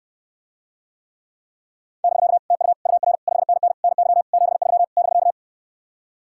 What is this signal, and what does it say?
Morse code, heard as a keyed tone, about 69 wpm, 700 Hz. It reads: <BK> TU UR 5NN NY 73 <BK>